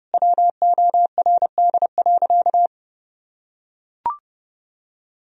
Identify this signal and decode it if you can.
Morse code, heard as a keyed tone, about 30 wpm, 700 Hz. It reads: WORD.
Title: Sentences from Top 1000 Words - Review Part 3 - 30wpm